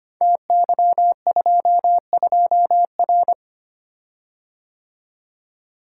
Morse code, heard as a keyed tone, about 25 wpm, 700 Hz. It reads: TY22R
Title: Code Groups - 5 Letter:Numbers - 25wpm